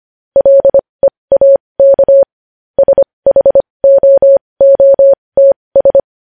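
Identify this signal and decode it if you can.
Morse code, heard as a keyed tone, about 25 wpm, spaced as for 20 wpm, 550 Hz. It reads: LEAK SHOOTS
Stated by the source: Morse code news headlines sent at 20.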